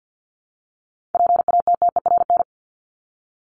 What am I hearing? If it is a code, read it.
Morse code, heard as a keyed tone, about 50 wpm, 700 Hz. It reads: PATTERN
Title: Top 500 Words Repeated 50wpm